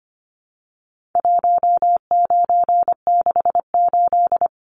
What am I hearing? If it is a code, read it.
Morse code, heard as a keyed tone, about 25 wpm, 700 Hz. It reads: 1968
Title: Numbers - 4-Digit 25wpm